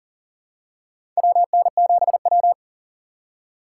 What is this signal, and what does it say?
Morse code, heard as a keyed tone, about 40 wpm, 700 Hz. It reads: WN7W
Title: USA Call Signs - 4x Character Spacing - 40wpm